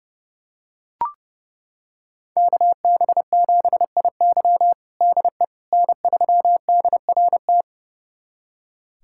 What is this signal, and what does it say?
Morse code, heard as a keyed tone, about 30 wpm, 700 Hz. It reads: KB7IY DE N3DRT
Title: USA Callsign DE Callsign - 5 Characters - 30wpm